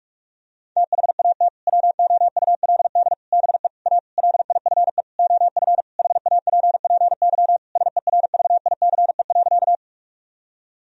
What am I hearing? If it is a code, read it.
Morse code, heard as a keyed tone, about 41 wpm, 700 Hz. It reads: THAT WOULD BE A LIFE OF HAPPY SERVICE.
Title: Sentences from Top 1000 Words - Review Part 4 - 45wpm